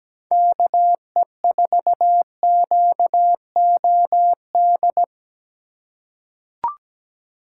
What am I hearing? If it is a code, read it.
Morse code, heard as a keyed tone, about 17 wpm, 700 Hz. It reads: KE4QOD
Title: USA Call Signs - 4x Character Spacing - 17wpm